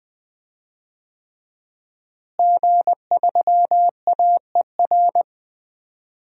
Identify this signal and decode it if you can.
Morse code, heard as a keyed tone, about 21 wpm, 700 Hz. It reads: G3AER
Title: DXCC-100 Call Signs - 5 Characters - Repeated 3x - 20wpm